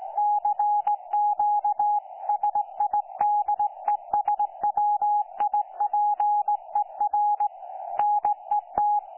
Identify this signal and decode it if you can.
Morse code, heard as a keyed tone, about 18 wpm, 830 Hz. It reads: CQ SIDESWIPER NET
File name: 09 38Z 18088.0kHz k3fef darrel